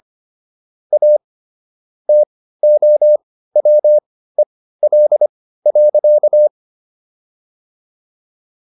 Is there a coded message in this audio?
Yes: Morse code from 0.9 to 6.5 s, heard as a keyed tone, about 25 wpm, spaced as for 15 wpm, 600 Hz. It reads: A TOWEL.